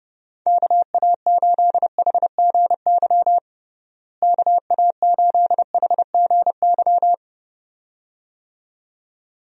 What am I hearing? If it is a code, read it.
Morse code, heard as a keyed tone, about 30 wpm, 700 Hz. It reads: KA8HGY KA8HGY
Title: USA Call Signs - 6 Characters - Repeated 2x - 30wpm